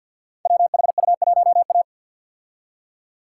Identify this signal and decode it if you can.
Morse code, heard as a keyed tone, about 52 wpm, 700 Hz. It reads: WHU1A
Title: Code Groups - 5 Letter:Numbers - 50wpm